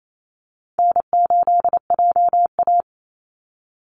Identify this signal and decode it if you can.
Morse code, heard as a keyed tone, about 28 wpm, 700 Hz. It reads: N8JA
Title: Easy Call Signs - 28wpm